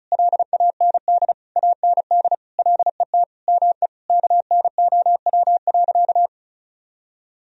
Morse code, heard as a keyed tone, about 34 wpm, 700 Hz. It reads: LAND AND LET ME KNOW.
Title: Sentences from Top 1000 Words - Review Part 2 - 35wpm